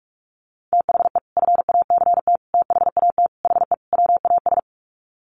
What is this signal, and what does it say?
Morse code, heard as a keyed tone, about 44 wpm, 700 Hz. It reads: THE FACT THAT HE WAS
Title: Sets of 5 Words 45wpm